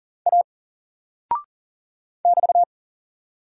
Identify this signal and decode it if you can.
Morse code, heard as a keyed tone, about 40 wpm, 700 Hz. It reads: A <BT>
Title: Single Character (Letter:Number:Punctuation:Prosign) 40wpm